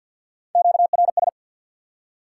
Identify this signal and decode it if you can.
Morse code, heard as a keyed tone, about 53 wpm, 700 Hz. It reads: QRS
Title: Top 180 QSO Elements 50wpm